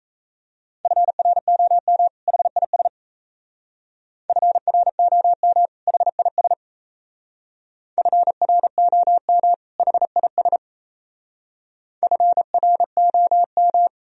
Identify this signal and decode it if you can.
Morse code, heard as a keyed tone, about 35 wpm, 700 Hz. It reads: FROM HIS FROM HIS FROM HIS FROM
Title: Speed-Racing - Sets of 2 Words 28wpm